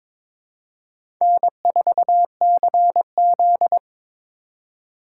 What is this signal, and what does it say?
Morse code, heard as a keyed tone, about 22 wpm, 700 Hz. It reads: N4CZ
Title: USA Call Signs - 4 Characters - 2x Character Spacing - 22wpm